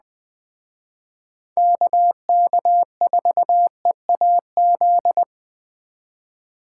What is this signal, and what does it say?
Morse code, heard as a keyed tone, about 20 wpm, 700 Hz. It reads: KK4EAZ